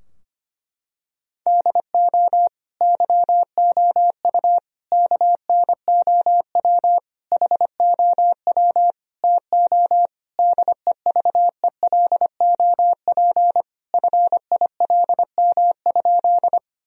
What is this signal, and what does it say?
Morse code, heard as a keyed tone, about 25 wpm, 700 Hz. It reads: DO YOU KNOW HOW TO DEVELOP FILM?
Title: Sentences from Top 1000 Words - Review Part 3 - 25wpm